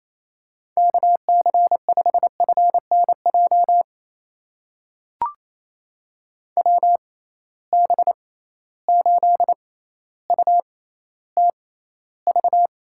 Morse code, heard as a keyed tone, about 28 wpm, spaced as for 14 wpm, 700 Hz. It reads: KC5FNJ WB8UTV
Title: USA Call Signs - 6 Characters - 6x Character Spacing - 28wpm